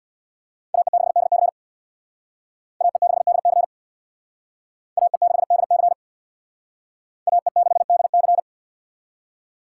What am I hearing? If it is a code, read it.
Morse code, heard as a keyed tone, about 61 wpm, 700 Hz. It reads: AE6DC AE6DC AE6DC AE6DC
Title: Speed Racing - USA 2x2 Call Signs 50wpm